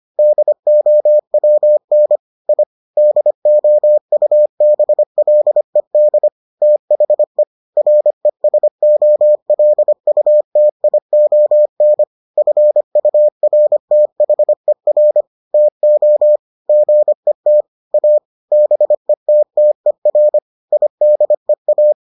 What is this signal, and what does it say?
Morse code, heard as a keyed tone, about 25 wpm, 600 Hz. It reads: DOWN I DOUBLED THE RESOLUTION FURTHER TO GET A BETTER IDEA